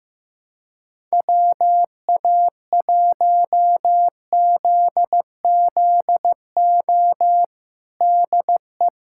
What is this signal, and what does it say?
Morse code, heard as a keyed tone, about 15 wpm, 700 Hz. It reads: WA1ZZO DE
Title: USA Callsign DE Callsign - 6 Characters - 15wpm